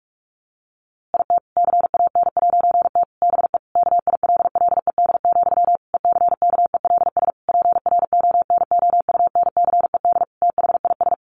Morse code, heard as a keyed tone, about 44 wpm, 700 Hz. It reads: IT CAN'T BE KILLED, ECKELS PRONOUNCED THIS
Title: A Sound Of Thunder 45wpm